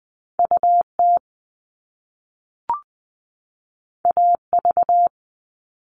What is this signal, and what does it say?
Morse code, heard as a keyed tone, about 20 wpm, 700 Hz. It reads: UT AV
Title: Top Two Letter Combinations in Words 20wpm